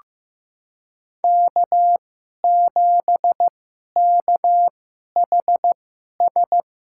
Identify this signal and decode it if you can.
Morse code, heard as a keyed tone, about 15 wpm, spaced as for 11 wpm, 700 Hz. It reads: K7KHS
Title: USA Call Signs - 2x Character Spacing - 15wpm